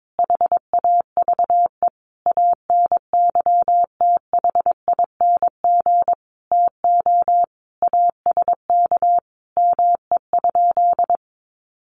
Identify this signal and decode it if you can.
Morse code, heard as a keyed tone, about 22 wpm, 700 Hz. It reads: HAVE ANYTHING TO ASK ME?
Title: Sentences from Top 1000 Words 22wpm